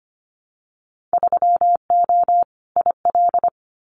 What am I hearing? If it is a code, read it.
Morse code, heard as a keyed tone, about 25 wpm, 700 Hz. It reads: 3O IL
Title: Winter Field Day Exchanges Encore 25wpm